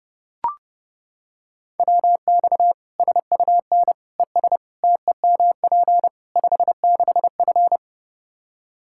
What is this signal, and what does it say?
Morse code, heard as a keyed tone, about 30 wpm, 700 Hz. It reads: WX SUN ES TEMP 56F
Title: Practice QSO - Exchange 2 - Info - 30wpm